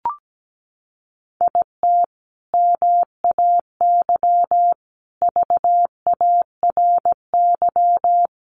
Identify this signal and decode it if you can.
Morse code, heard as a keyed tone, about 17 wpm, 700 Hz. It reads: IT MAY VARY
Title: Sentences from Top 1200 Words 17wpm